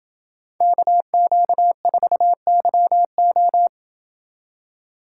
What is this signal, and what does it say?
Morse code, heard as a keyed tone, about 27 wpm, 700 Hz. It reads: KQ4YO